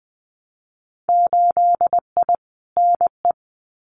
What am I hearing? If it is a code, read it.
Morse code, heard as a keyed tone, about 20 wpm, 700 Hz. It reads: 8I NE
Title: Winter Field Day Exchanges Encore 20wpm